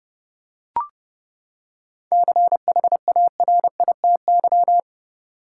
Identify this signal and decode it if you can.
Morse code, heard as a keyed tone, about 31 wpm, 700 Hz. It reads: CHARITY
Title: Seven Letter Words - 30wpm